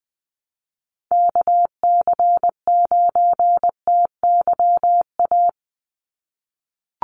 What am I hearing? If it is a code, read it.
Morse code, heard as a keyed tone, about 20 wpm, 700 Hz. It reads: KC9TYA
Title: NA SOTA Chaser Callsigns 20wpm